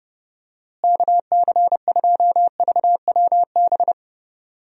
Morse code, heard as a keyed tone, about 30 wpm, 700 Hz. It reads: KC2VWB
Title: USA Call Signs - 6 Characters - Repeated 3x - 30wpm